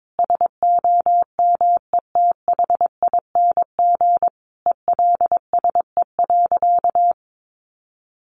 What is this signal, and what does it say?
Morse code, heard as a keyed tone, about 22 wpm, 700 Hz. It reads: SOMETHING ELSE.
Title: Sentences from Top 800 Words 22wpm